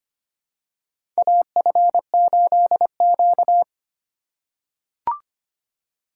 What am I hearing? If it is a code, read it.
Morse code, heard as a keyed tone, about 25 wpm, 700 Hz. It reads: AF8Q
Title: USA Call Signs - 4 Characters - 2x Character Spacing - 25wpm